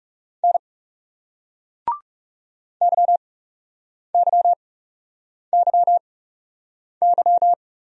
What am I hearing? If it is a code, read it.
Morse code, heard as a keyed tone, about 39 wpm, spaced as for 8 wpm, 700 Hz. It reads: N YYYY